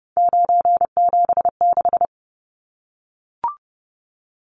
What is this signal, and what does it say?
Morse code, heard as a keyed tone, about 30 wpm, 700 Hz. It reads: 976
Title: Numbers - 3-Digit 30wpm